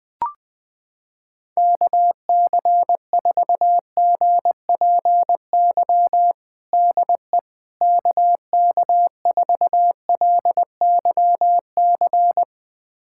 Morse code, heard as a keyed tone, about 20 wpm, 700 Hz. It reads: KC4GPY DE KK4LYC
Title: USA Callsign DE Callsign - 6 Characters - 20wpm